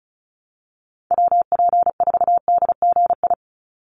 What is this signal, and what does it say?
Morse code, heard as a keyed tone, about 34 wpm, 700 Hz. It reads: WP4DGI
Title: USA Call Signs 35wpm